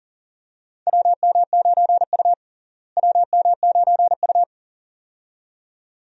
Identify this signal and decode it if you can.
Morse code, heard as a keyed tone, about 37 wpm, 700 Hz. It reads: WM9U WM9U
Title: USA Call Signs - 4 Characters - Repeated 2x - 40wpm